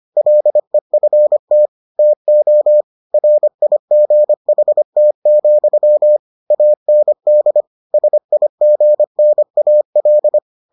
Morse code, heard as a keyed tone, about 25 wpm, 600 Hz. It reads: LEFT TO RIGHT, AND SIGNAL